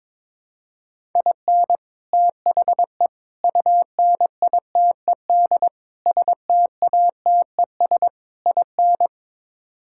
Morse code, heard as a keyed tone, about 22 wpm, 700 Hz. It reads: IN THE UNITED STATES IN
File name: practice / 22wpm / Build A Phrase - 5-Word Phrases - 22wpm